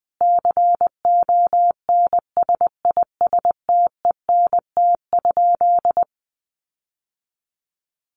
Morse code, heard as a keyed tone, about 20 wpm, 700 Hz. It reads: CONSISTENT?
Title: Sentences from Top 1500 Words 20wpm